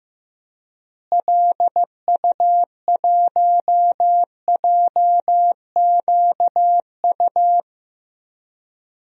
Morse code, heard as a keyed tone, about 15 wpm, 700 Hz. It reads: LU1JQU
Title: DXCC-100 Call Signs - 6 Characters - Repeated 3x - 15wpm